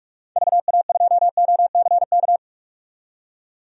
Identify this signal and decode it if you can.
Morse code, heard as a keyed tone, about 41 wpm, 700 Hz. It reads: VA2OCK
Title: DXCC-100 Call Signs - 6 Characters - 45wpm